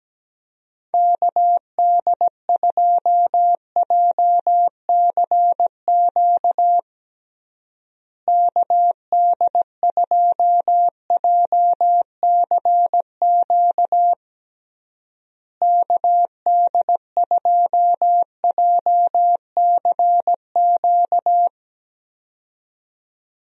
Morse code, heard as a keyed tone, about 17 wpm, 700 Hz. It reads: KD2JCQ KD2JCQ KD2JCQ
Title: USA Call Signs - 6 Characters - Repeated 3x - 17wpm